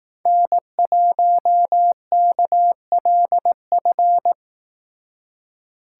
Morse code, heard as a keyed tone, about 18 wpm, 700 Hz. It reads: N1KLF
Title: Speed Racing - USA 1x3 Call Signs 15wpm